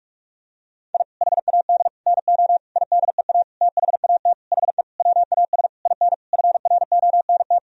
Morse code, heard as a keyed tone, about 46 wpm, 700 Hz. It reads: I HAD NO IDEA THAT HE WAS IN FRONT
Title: Sentences from Top 1000 Words - Review Part 2 - 45wpm